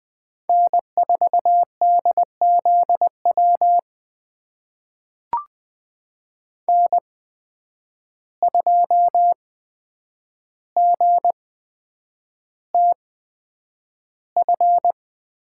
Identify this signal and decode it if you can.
Morse code, heard as a keyed tone, about 20 wpm, spaced as for 9 wpm, 700 Hz. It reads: N4DZW N2GTF
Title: USA Call Signs - 5 Characters - 8x Character Spacing - 20wpm